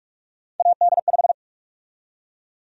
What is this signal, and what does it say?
Morse code, heard as a keyed tone, about 44 wpm, 700 Hz. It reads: AD5
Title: USA Call Sign Prefixes - 45wpm